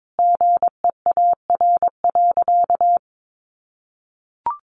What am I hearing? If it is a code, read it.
Morse code, heard as a keyed tone, about 22 wpm, 700 Hz. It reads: GEAR.
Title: Sentences from Top 1400 Words 22wpm